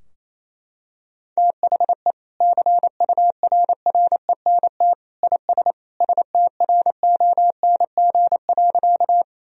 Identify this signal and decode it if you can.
Morse code, heard as a keyed tone, about 28 wpm, 700 Hz. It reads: THE CURRENT IS STRONG.
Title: Sentences from Top 1000 Words - Review Part 4 - 28wpm